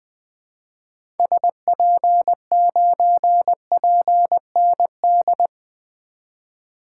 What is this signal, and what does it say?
Morse code, heard as a keyed tone, about 21 wpm, 700 Hz. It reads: SP9PND
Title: 2023 SOTA Activator Call Signs - 20wpm